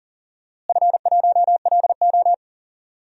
Morse code, heard as a keyed tone, about 40 wpm, 700 Hz. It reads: F1LO
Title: DXCC-100 Call Signs - 4 Characters - Repeated 2x - 40wpm